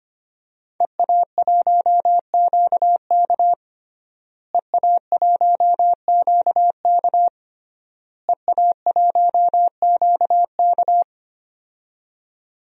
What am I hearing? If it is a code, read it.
Morse code, heard as a keyed tone, about 25 wpm, 700 Hz. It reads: EA1QK EA1QK EA1QK
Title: DXCC-100 Call Signs - 5 Characters - Repeated 3x - 25wpm